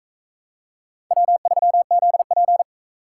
Morse code, heard as a keyed tone, about 44 wpm, 700 Hz. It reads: W3ZP